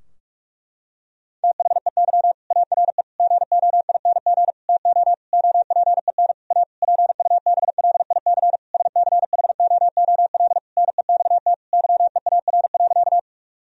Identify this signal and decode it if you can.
Morse code, heard as a keyed tone, about 46 wpm, 700 Hz. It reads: THEY ARE GOING TO OPEN A PUBLIC SCHOOL NEXT YEAR.
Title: Sentences from Top 1000 Words - Review Part 3 - 45wpm